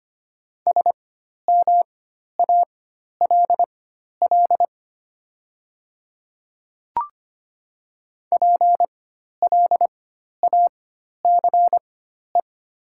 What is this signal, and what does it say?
Morse code, heard as a keyed tone, about 25 wpm, spaced as for 12 wpm, 700 Hz. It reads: SMALL PLACE
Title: 4x Character Spacing - Top 200 Words - 25wpm